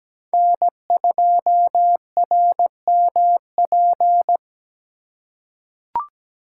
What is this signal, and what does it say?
Morse code, heard as a keyed tone, about 17 wpm, 700 Hz. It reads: N2RMP
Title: USA Call Signs - 5 Characters - 4x Character Spacing - 17wpm